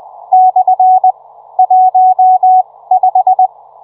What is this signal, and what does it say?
Morse code, heard as a keyed tone, about 20 wpm, spaced as for 16 wpm, 740 Hz. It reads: / 15